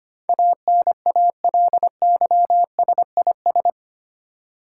Morse code, heard as a keyed tone, about 25 wpm, 700 Hz. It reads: ANALYSIS
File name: 8 Letter Words - First 3 Letters Given - 25wpm